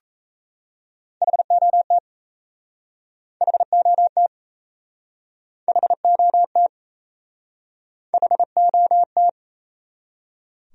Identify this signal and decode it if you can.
Morse code, heard as a keyed tone, about 39 wpm, 700 Hz. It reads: HOT HOT HOT HOT